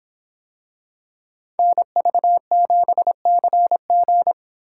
Morse code, heard as a keyed tone, about 25 wpm, 700 Hz. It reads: NV7CG